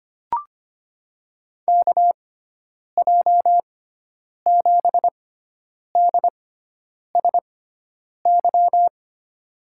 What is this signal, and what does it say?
Morse code, heard as a keyed tone, about 25 wpm, spaced as for 9 wpm, 700 Hz. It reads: KJ7DSY
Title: USA Call Signs - 6x Character Spacing - 25wpm